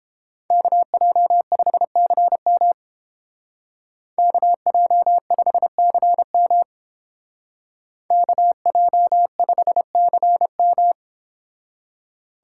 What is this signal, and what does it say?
Morse code, heard as a keyed tone, about 30 wpm, 700 Hz. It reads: KJ5CM KJ5CM KJ5CM